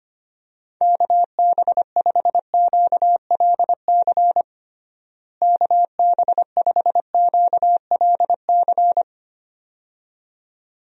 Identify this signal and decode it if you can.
Morse code, heard as a keyed tone, about 25 wpm, 700 Hz. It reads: KB5QLC KB5QLC
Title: USA Call Signs - 6 Characters - Repeated 2x - 25wpm